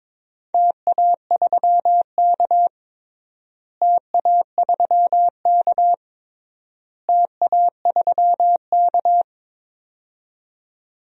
Morse code, heard as a keyed tone, about 22 wpm, 700 Hz. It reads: TA3K TA3K TA3K
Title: DXCC-100 Call Signs - 4 Characters - Repeated 3x - 22wpm